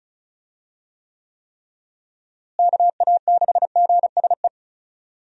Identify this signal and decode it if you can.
Morse code, heard as a keyed tone, about 36 wpm, 700 Hz. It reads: KA6GSE
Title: USA Call Signs - 6 Characters - Repeated 3x - 35wpm